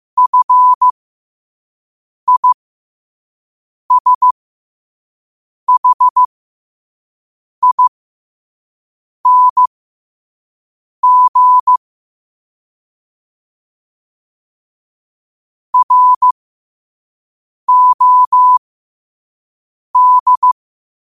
Morse code, heard as a keyed tone, about 15 wpm, spaced as for 5 wpm, 1000 Hz. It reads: FISHING ROD